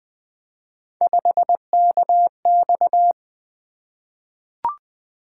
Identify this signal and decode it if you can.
Morse code, heard as a keyed tone, about 20 wpm, 700 Hz. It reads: 5KX
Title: Three Letter-Number Combinations - 20wpm